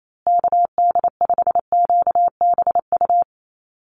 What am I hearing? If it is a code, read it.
Morse code, heard as a keyed tone, about 28 wpm, 700 Hz. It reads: KD5QBU
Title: USA Call Signs Repeated 3x 28wpm